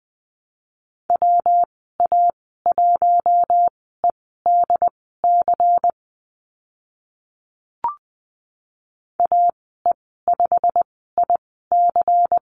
Morse code, heard as a keyed tone, about 20 wpm, spaced as for 14 wpm, 700 Hz. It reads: WA1EDC AE5IC
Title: 193 Lesson 31g - Letter v - Call Signs 20wpm